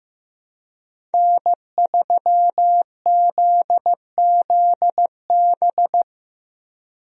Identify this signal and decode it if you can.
Morse code, heard as a keyed tone, about 15 wpm, 700 Hz. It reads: N3ZZB
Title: USA Call Signs - 5 Characters - Repeated 3x - 15wpm